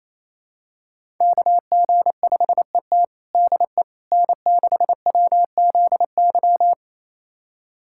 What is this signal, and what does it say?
Morse code, heard as a keyed tone, about 28 wpm, 700 Hz. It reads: KG5ET DE N6WZY
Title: USA Callsign DE Callsign - 5 Characters - 28wpm